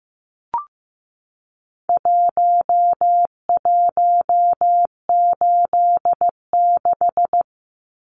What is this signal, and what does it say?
Morse code, heard as a keyed tone, about 15 wpm, 700 Hz. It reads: 1186
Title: Numbers - Contest Style - 4-Digit 15wpm